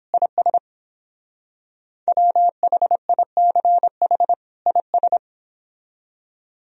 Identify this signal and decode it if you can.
Morse code, heard as a keyed tone, about 28 wpm, 700 Hz. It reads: IS WHICH IS